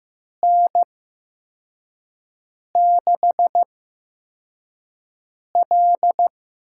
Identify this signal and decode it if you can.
Morse code, heard as a keyed tone, about 15 wpm, spaced as for 4 wpm, 700 Hz. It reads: N6L
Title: USA Call Signs - 5 Characters - 8x Character Spacing - 15wpm